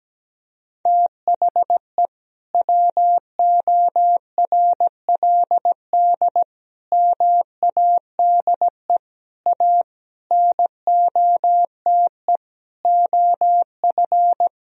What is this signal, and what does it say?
Morse code, heard as a keyed tone, about 17 wpm, 700 Hz. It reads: THE WORLD MADE A NOTE OF
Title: Sentences from Top 1000 Words - Review Part 2 - 17wpm